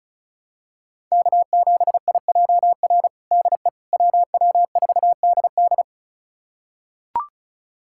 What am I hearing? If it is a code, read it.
Morse code, heard as a keyed tone, about 34 wpm, 700 Hz. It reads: K7IJR DE WW4DD
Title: USA Callsign DE Callsign - 5 Characters - 35wpm